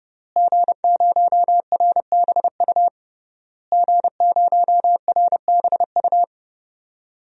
Morse code, heard as a keyed tone, about 30 wpm, 700 Hz. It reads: G0RBU G0RBU
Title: DXCC-100 Call Signs - 5 Characters - Repeated 3x - 30wpm